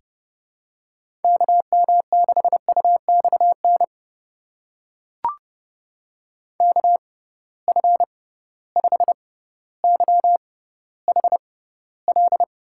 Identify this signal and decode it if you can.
Morse code, heard as a keyed tone, about 30 wpm, spaced as for 15 wpm, 700 Hz. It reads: KM6UXN KF5YHL
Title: USA Call Signs - 6 Characters - 6x Character Spacing - 30wpm